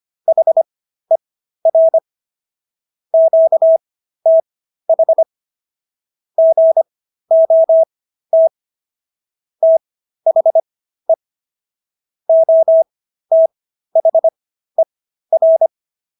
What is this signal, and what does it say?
Morse code, heard as a keyed tone, about 25 wpm, spaced as for 13 wpm, 650 Hz. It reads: HER QTH GOT THE OTHER